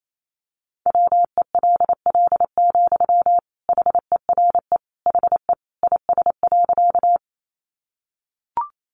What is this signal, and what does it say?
Morse code, heard as a keyed tone, about 28 wpm, 700 Hz. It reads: WELL, HERE HE IS.
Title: Sentences from Top 200 Words 28wpm